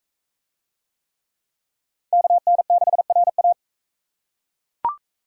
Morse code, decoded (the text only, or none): KN6RA